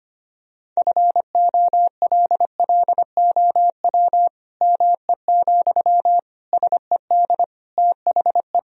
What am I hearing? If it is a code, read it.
Morse code, heard as a keyed tone, about 25 wpm, 700 Hz. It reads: FOLLOW ME, SED THE
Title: Sentences from Top 200 Words and Abbreviations 25wpm